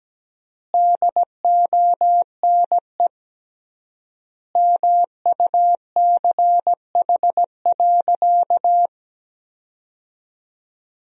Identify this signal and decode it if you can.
Morse code, heard as a keyed tone, about 17 wpm, 700 Hz. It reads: DONE MUCH.
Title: 3x Word-Spacing - Sentences from Top 500 Words - 17wpm